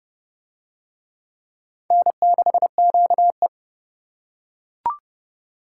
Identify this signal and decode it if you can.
Morse code, heard as a keyed tone, about 28 wpm, 700 Hz. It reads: N6QE